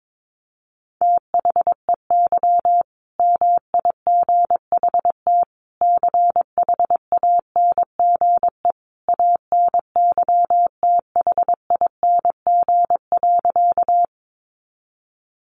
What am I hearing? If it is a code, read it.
Morse code, heard as a keyed tone, about 22 wpm, 700 Hz. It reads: THEY MIGHT CHANGE ANYTHING.
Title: A Sound Of Thunder 22wpm